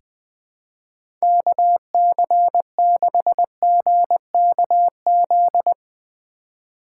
Morse code, heard as a keyed tone, about 21 wpm, 700 Hz. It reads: KC6GKZ
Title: USA Call Signs - 4x Character Spacing - 20wpm